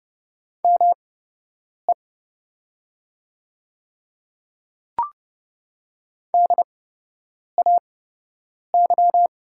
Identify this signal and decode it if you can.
Morse code, heard as a keyed tone, about 30 wpm, spaced as for 8 wpm, 700 Hz. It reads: ME DAY
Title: 8x Character Spacing - Top 200 Words - 30wpm